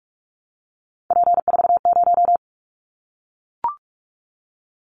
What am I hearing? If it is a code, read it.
Morse code, heard as a keyed tone, about 44 wpm, 700 Hz. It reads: P40
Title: Callsign Prefixes for Top 100 DXCC - 45wpm